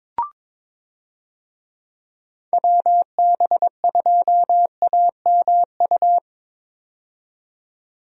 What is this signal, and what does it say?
Morse code, heard as a keyed tone, about 22 wpm, 700 Hz. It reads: WB2AMU
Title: USA Call Signs - 6 Characters - 22wpm